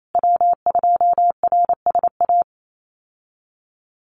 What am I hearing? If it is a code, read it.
Morse code, heard as a keyed tone, about 28 wpm, 700 Hz. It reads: W2RSA
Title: USA Call Signs Repeated 3x 28wpm